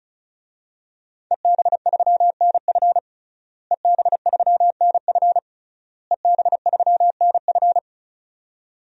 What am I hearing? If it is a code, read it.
Morse code, heard as a keyed tone, about 36 wpm, 700 Hz. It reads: EB3NF EB3NF EB3NF
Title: DXCC-100 Call Signs - 5 Characters - Repeated 3x - 35wpm